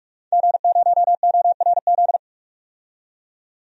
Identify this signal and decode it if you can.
Morse code, heard as a keyed tone, about 41 wpm, 700 Hz. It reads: G0ORZ
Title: DXCC-100 Call Signs - 5 Characters - 45wpm